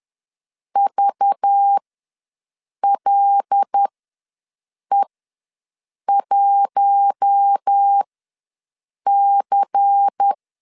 Morse code, heard as a keyed tone, about 10 wpm, spaced as for 6 wpm, 800 Hz. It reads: VLE1C
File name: LEZIONI / 5 LBVC / 5 LBVC ws 10wpm 3733